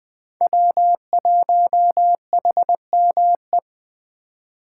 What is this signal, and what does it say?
Morse code, heard as a keyed tone, about 24 wpm, 700 Hz. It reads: W1HME
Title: Speed Racing - USA 1x3 Call Signs 20wpm